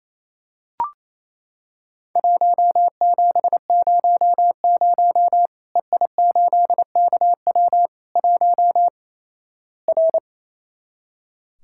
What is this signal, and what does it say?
Morse code, heard as a keyed tone, about 28 wpm, 700 Hz. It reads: 1700 EI8KW 1 R
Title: Worked All Europe DX Contest - QTC Exchange - 28wpm